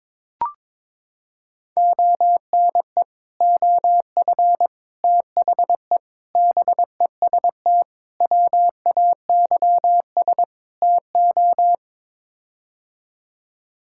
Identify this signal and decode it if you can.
Morse code, heard as a keyed tone, about 22 wpm, 700 Hz. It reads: ONE OF THE BEST WAYS TO
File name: practice / 22wpm / Sets of 6 Words - 22wpm